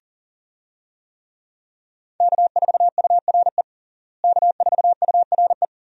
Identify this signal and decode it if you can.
Morse code, heard as a keyed tone, about 40 wpm, 700 Hz. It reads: K4URE K4URE
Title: USA Call Signs - 5 Characters - Repeated 3x - 40wpm